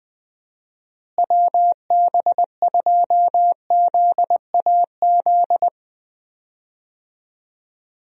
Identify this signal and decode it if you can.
Morse code, heard as a keyed tone, about 20 wpm, 700 Hz. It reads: WB2ZAZ
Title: USA Call Signs - 6 Characters - Repeated 3x - 20wpm